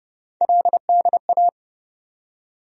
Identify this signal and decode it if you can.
Morse code, heard as a keyed tone, about 29 wpm, 700 Hz. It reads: LDA
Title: USA Call Sign Postfixes - 3 Letters 30wpm